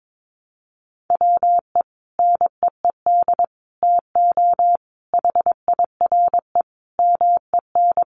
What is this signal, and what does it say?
Morse code, heard as a keyed tone, about 22 wpm, 700 Hz. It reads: WE NEED TO HIRE MEN
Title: Sentences from Top 1400 Words 22wpm